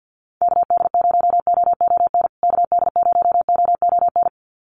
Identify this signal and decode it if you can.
Morse code, heard as a keyed tone, about 50 wpm, 700 Hz. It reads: KD0OON KD0OON
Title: USA Call Signs Repeated 2x 50wpm